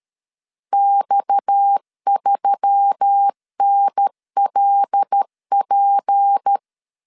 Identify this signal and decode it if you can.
Morse code, heard as a keyed tone, about 13 wpm, 800 Hz. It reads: X3NLP
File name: CW / DET / 1 6 1500C S A 12wpm 3440